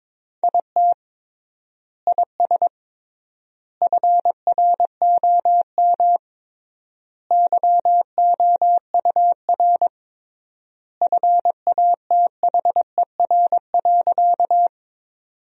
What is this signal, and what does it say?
Morse code, heard as a keyed tone, about 22 wpm, 700 Hz. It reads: IT IS FROM YOUR FATHER.
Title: 3x Word-Spacing - Sentences from Top 300 Words - 22wpm